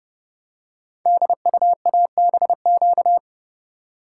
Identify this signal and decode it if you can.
Morse code, heard as a keyed tone, about 30 wpm, 700 Hz. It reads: DUABQ
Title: Code Groups - 5 Letters - 30wpm